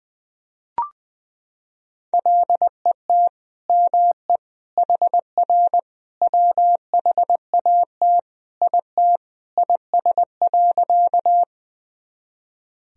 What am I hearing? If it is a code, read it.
Morse code, heard as a keyed tone, about 21 wpm, 700 Hz. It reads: LET ME HR WHAT IT IS.
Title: Sentences from Top 300 Words and Abbreviations 20wpm